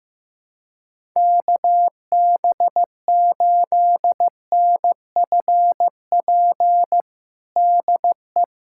Morse code, heard as a keyed tone, about 15 wpm, 700 Hz. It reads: KB8NFP DE
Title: USA Callsign DE Callsign - 6 Characters - 15wpm